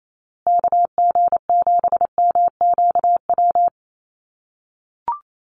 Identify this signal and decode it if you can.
Morse code, heard as a keyed tone, about 28 wpm, 700 Hz. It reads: KG7MQW
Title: USA Call Signs Repeated 2x 28wpm